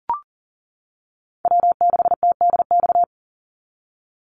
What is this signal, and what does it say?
Morse code, heard as a keyed tone, about 40 wpm, 700 Hz. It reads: W6TDX
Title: NA SOTA Chaser Callsigns 40wpm